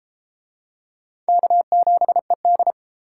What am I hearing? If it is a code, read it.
Morse code, heard as a keyed tone, about 28 wpm, 700 Hz. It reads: K7ED